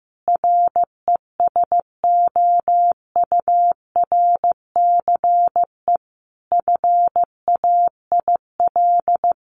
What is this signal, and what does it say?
Morse code, heard as a keyed tone, about 15 wpm, 700 Hz. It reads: RESOURCE FAIL
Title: Sentences from Top 1300 Words 15wpm